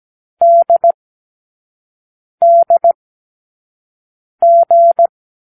Morse code, heard as a keyed tone, about 17 wpm, 680 Hz. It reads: D D G
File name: k7qo / Lesson12(A-H)